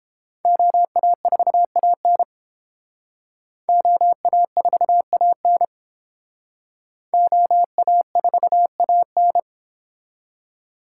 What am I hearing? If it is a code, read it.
Morse code, heard as a keyed tone, about 31 wpm, 700 Hz. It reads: OA4AN OA4AN OA4AN